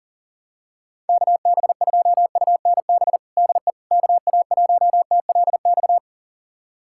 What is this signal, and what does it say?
Morse code, heard as a keyed tone, about 40 wpm, 700 Hz. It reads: KB2UNB DE KA1TLX
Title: USA Callsign DE Callsign - 6 Characters - 40wpm